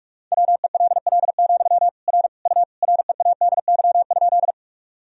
Morse code, heard as a keyed tone, about 42 wpm, 700 Hz. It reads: WELL, R U READY?
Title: Sentences from Top 400 Words and Abbreviations 45wpm